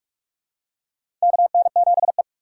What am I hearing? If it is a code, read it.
Morse code, heard as a keyed tone, about 46 wpm, 700 Hz. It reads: KN7E
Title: USA Call Signs - 4 Characters - 3x Character Spacing - 45wpm